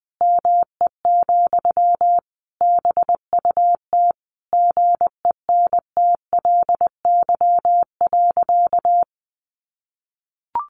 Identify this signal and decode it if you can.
Morse code, heard as a keyed tone, about 20 wpm, 700 Hz. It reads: ME, BUT GENTLY.
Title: Sentences from Top 1500 Words 20wpm